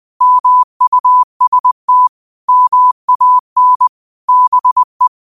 Morse code, heard as a keyed tone, about 20 wpm, 1000 Hz. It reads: MUST MAN BE
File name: qotdfiles / 20wpm